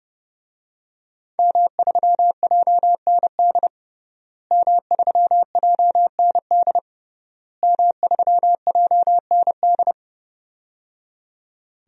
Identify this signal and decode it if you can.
Morse code, heard as a keyed tone, about 30 wpm, 700 Hz. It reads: M3JND M3JND M3JND
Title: DXCC-100 Call Signs - 5 Characters - Repeated 3x - 30wpm